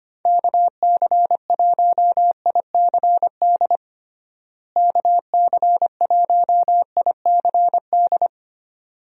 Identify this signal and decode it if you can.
Morse code, heard as a keyed tone, about 25 wpm, 700 Hz. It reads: KC1ICD KC1ICD
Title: USA Call Signs - 6 Characters - Repeated 2x - 25wpm